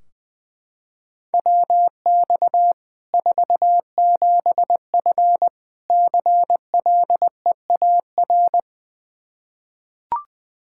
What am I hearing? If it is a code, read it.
Morse code, heard as a keyed tone, about 20 wpm, 700 Hz. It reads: WX 47F CLEAR
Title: Practice QSO - Exchange 2 - Info - 20wpm